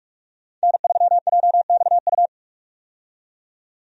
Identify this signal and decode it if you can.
Morse code, heard as a keyed tone, about 46 wpm, 700 Hz. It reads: N3JXU
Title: USA Call Signs - 5 Characters - 45wpm